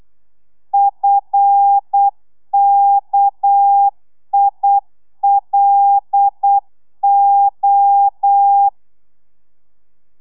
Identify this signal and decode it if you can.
Morse code, heard as a keyed tone, about 8 wpm, 800 Hz. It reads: FKILO